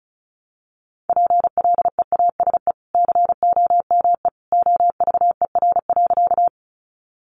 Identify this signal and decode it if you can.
Morse code, heard as a keyed tone, about 34 wpm, 700 Hz. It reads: PLEASE COME OVER.
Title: Sentences from Top 1500 Words 35wpm